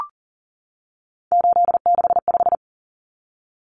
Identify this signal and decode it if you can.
Morse code, heard as a keyed tone, about 38 wpm, 700 Hz. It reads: 865
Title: Common Cut Numbers - 3-Digit 40wpm